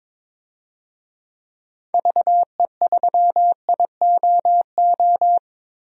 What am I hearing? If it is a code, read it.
Morse code, heard as a keyed tone, about 22 wpm, 700 Hz. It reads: VE3IOO
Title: DXCC-100 Call Signs - 6 Characters - 22wpm